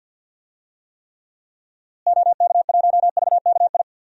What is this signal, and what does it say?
Morse code, heard as a keyed tone, about 53 wpm, 700 Hz. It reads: OK1VKI
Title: DXCC-100 Call Signs - 6 Characters - Repeated 2x - 50wpm